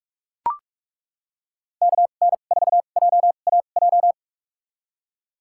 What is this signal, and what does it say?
Morse code, heard as a keyed tone, about 42 wpm, spaced as for 31 wpm, 700 Hz. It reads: KN4JAJ
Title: USA Call Signs - 6 Characters - 2x Character Spacing - 45wpm